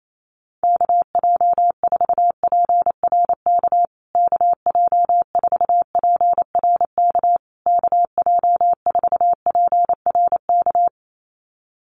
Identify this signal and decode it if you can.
Morse code, heard as a keyed tone, about 28 wpm, 700 Hz. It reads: KJ4PRK KJ4PRK KJ4PRK
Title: USA Call Signs Repeated 3x 28wpm